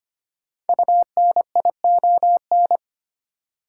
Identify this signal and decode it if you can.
Morse code, heard as a keyed tone, about 25 wpm, 700 Hz. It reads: UNION
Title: Sentences from Top 1400 Words - New Word Drill - 25wpm